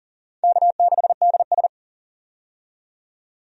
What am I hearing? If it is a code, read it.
Morse code, heard as a keyed tone, about 40 wpm, 700 Hz. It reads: K6DS
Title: USA Call Signs - 4 Characters - 40wpm